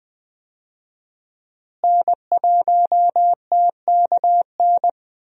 Morse code, heard as a keyed tone, about 20 wpm, 700 Hz. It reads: N1TKN
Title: USA Call Signs - 5 Characters - Repeated 2x - 20wpm